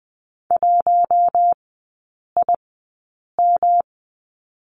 Morse code, heard as a keyed tone, about 20 wpm, spaced as for 8 wpm, 700 Hz. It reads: 1IM
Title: 138 Lesson 23e - Letter f - Three Characters 20wpm